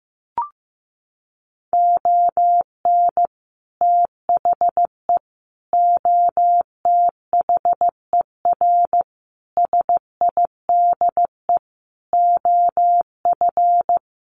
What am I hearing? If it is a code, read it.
Morse code, heard as a keyed tone, about 15 wpm, 700 Hz. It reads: ON THE OTHER SIDE OF
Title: Sets of 5 Words 15wpm